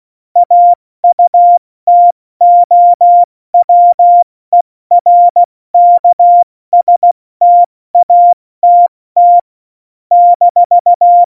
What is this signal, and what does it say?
Morse code, heard as a keyed tone, about 16 wpm, 700 Hz. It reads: AUTOWERKSTATT -
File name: Morsekurs / Witze / Witz 02 Juni 2020, 70 F80